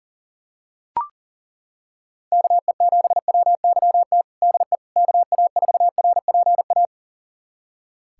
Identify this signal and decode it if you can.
Morse code, heard as a keyed tone, about 37 wpm, 700 Hz. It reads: KE7WYT DE KA4RPA
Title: USA Callsign DE Callsign 40wpm